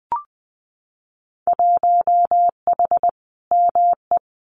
Morse code, heard as a keyed tone, about 20 wpm, 700 Hz. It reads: 1H ME
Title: Winter Field Day Exchanges 20wpm